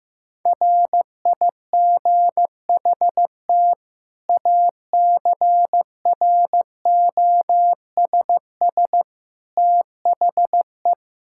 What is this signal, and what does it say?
Morse code, heard as a keyed tone, about 15 wpm, 700 Hz. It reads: RIGHT ACROSS THE
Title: Sentences from Top 1000 Words - Review Part 5 - 15wpm